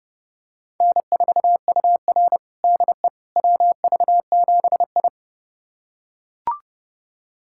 Morse code, heard as a keyed tone, about 30 wpm, 700 Hz. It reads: N4UR DE WV7I
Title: USA Callsign DE Callsign - 4 Characters - 30wpm